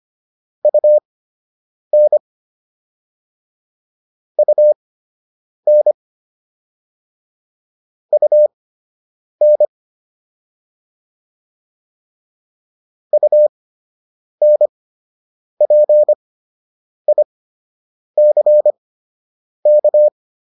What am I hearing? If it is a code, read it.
Morse code, heard as a keyed tone, about 25 wpm, spaced as for 8 wpm, 600 Hz. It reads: UN UN UN UNPICK